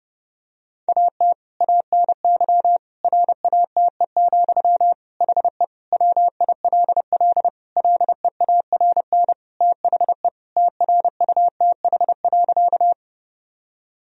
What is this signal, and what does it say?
Morse code, heard as a keyed tone, about 30 wpm, 700 Hz. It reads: AT ANY RATE, HE WILL LEARN THE TRUTH.
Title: Sentences from Top 1000 Words - Review Part 4 - 30wpm